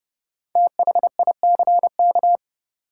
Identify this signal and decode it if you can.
Morse code, heard as a keyed tone, about 29 wpm, 700 Hz. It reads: THICK
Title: Five Letter Words - 30wpm